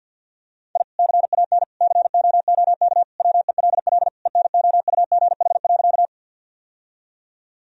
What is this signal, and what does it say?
Morse code, heard as a keyed tone, about 50 wpm, 700 Hz. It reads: I CAN COOK WELL ENOUGH.
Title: Sentences from Top 1000 Words - Review Part 4 - 50wpm